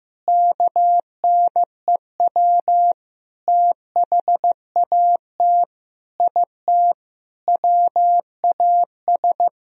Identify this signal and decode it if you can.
Morse code, heard as a keyed tone, about 15 wpm, 700 Hz. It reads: KNEW THAT IT WAS
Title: Sentences from Top 1000 Words - Review Part 2 - 15wpm